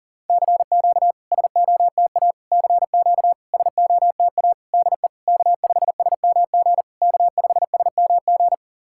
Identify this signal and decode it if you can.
Morse code, heard as a keyed tone, about 40 wpm, 690 Hz. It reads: CQ SOTA CQ SOTA DE K5SMG K5SMG
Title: SOTA QSO Practice 40wpm